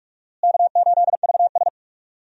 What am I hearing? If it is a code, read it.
Morse code, heard as a keyed tone, about 42 wpm, 700 Hz. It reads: K8VS
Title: USA Call Signs - 4 Characters - Repeated 3x - 45wpm